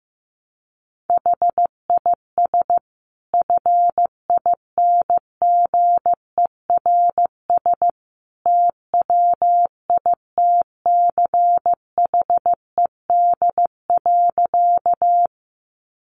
Morse code, heard as a keyed tone, about 14 wpm, 700 Hz. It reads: HIS FINGERS TWITCHED.
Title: A Sound Of Thunder 15wpm